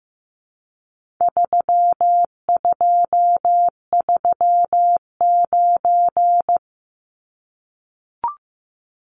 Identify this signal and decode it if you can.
Morse code, heard as a keyed tone, about 15 wpm, 700 Hz. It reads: 3239
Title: Numbers - 4-Digit 15wpm